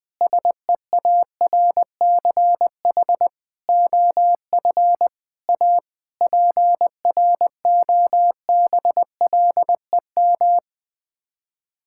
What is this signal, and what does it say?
Morse code, heard as a keyed tone, about 20 wpm, 700 Hz. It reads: SEARCH OF A PROBLEM
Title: Sets of 8 Words - 20wpm